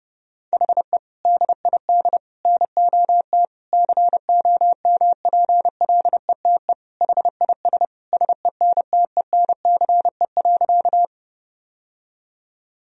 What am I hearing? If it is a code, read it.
Morse code, heard as a keyed tone, about 30 wpm, 700 Hz. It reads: HE DID NOT COMPLETE HIS SENTENCE.
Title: Sentences from Top 1000 Words - Review Part 1 - 30wpm